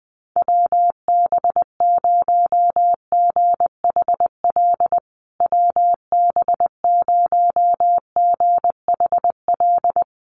Morse code, heard as a keyed tone, about 20 wpm, 700 Hz. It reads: WB0GHL WB0GHL
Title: USA Call Signs Repeated 2x 20wpm